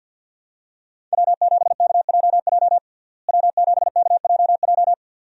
Morse code, heard as a keyed tone, about 50 wpm, 700 Hz. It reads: W7KJJ W7KJJ
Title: USA Call Signs - 5 Characters - Repeated 2x - 50wpm